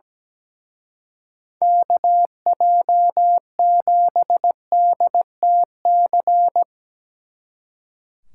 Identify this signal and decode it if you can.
Morse code, heard as a keyed tone, about 19 wpm, 700 Hz. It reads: KJ7DTC
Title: Speed Racing - USA 2x3 Call Signs 17wpm